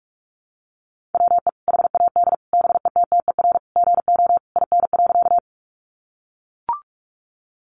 Morse code, heard as a keyed tone, about 44 wpm, 700 Hz. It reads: WE HAD BETTER GO IN.
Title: Sentences from Top 400 Words 45wpm